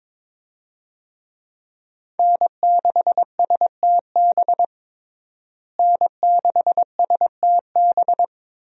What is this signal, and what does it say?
Morse code, heard as a keyed tone, about 22 wpm, 700 Hz. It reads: N6STB N6STB
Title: USA Call Signs - 5 Characters - Repeated 2x - 22wpm